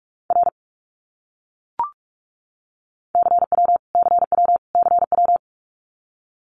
Morse code, heard as a keyed tone, about 44 wpm, 700 Hz. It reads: R CW CW CW
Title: Abbreviations Repeated 3x - 45wpm